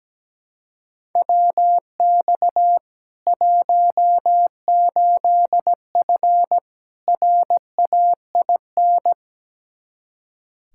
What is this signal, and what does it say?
Morse code, heard as a keyed tone, about 17 wpm, 700 Hz. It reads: WX 18F RAIN
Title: Practice QSO - Exchange 2 - Info - 17wpm